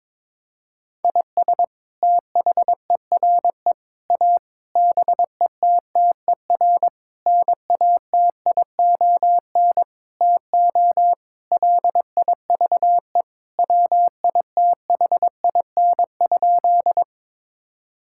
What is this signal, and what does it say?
Morse code, heard as a keyed tone, about 22 wpm, 700 Hz. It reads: IS THERE A BETTER NATION TO LIVE WITHIN?
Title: Sentences from Top 1000 Words - Review Part 6 - 22wpm